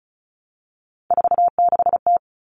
Morse code, heard as a keyed tone, about 34 wpm, 700 Hz. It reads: 46T
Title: All Cut Numbers - 3-Digit 35wpm